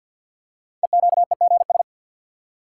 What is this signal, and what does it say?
Morse code, heard as a keyed tone, about 50 wpm, 690 Hz. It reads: EQEGS
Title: Code Groups - 5 Letters - 50wpm